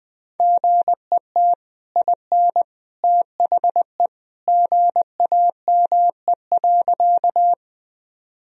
Morse code, heard as a keyed tone, about 21 wpm, 700 Hz. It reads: GET IN THE GAME.
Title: Sentences from Top 1000 Words - Review Part 3 - 20wpm